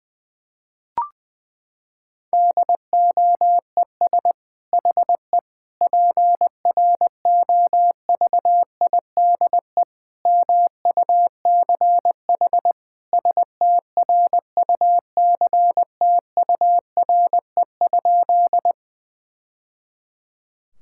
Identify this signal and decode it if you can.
Morse code, heard as a keyed tone, about 20 wpm, 700 Hz. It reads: DOES HE PROVIDE MUCH STRUCTURE?
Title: Sentences from Top 1000 Words - Review Part 6 - 20wpm